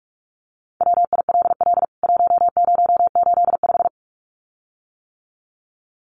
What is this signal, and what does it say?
Morse code, heard as a keyed tone, about 44 wpm, 700 Hz. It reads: WILL 1085
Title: CWT Exchange 45wpm